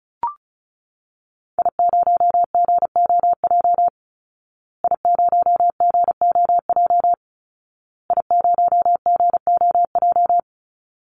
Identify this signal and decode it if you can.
Morse code, heard as a keyed tone, about 34 wpm, 700 Hz. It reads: I0GOJ I0GOJ I0GOJ
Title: Call Signs Repeated 2x - 35wpm